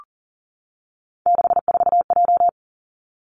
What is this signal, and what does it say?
Morse code, heard as a keyed tone, about 38 wpm, 700 Hz. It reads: 64J
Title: Three Challenging Letter-Number Combinations 40wpm